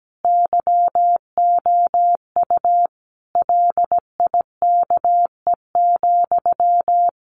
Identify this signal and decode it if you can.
Morse code, heard as a keyed tone, about 17 wpm, 700 Hz. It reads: YOU LIKE,
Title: Sentences from Top 1400 Words 17wpm